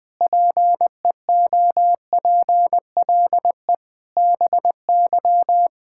Morse code, heard as a keyed tone, about 20 wpm, 700 Hz. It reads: PEOPLE BY
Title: Sets of 8 Words - 20wpm